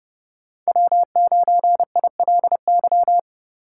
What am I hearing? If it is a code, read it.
Morse code, heard as a keyed tone, about 30 wpm, 700 Hz. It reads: W9ILY
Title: K1USN SST QSO Practice - 30wpm